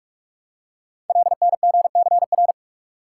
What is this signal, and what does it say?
Morse code, heard as a keyed tone, about 44 wpm, 700 Hz. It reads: LNGCR
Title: Code Groups - 5 Letters - 45wpm